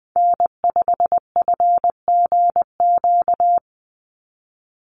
Morse code, heard as a keyed tone, about 20 wpm, 700 Hz. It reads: N5FGQ
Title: USA Call Signs 20wpm